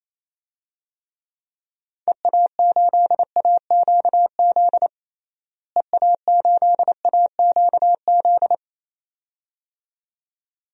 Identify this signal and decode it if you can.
Morse code, heard as a keyed tone, about 28 wpm, 700 Hz. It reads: EA8AQZ EA8AQZ
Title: DXCC-100 Call Signs - 6 Characters - Repeated 2x - 28wpm